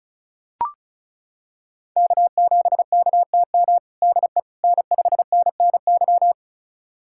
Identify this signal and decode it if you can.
Morse code, heard as a keyed tone, about 36 wpm, 700 Hz. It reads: K7KTM DE N5NNY
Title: USA Callsign DE Callsign 35wpm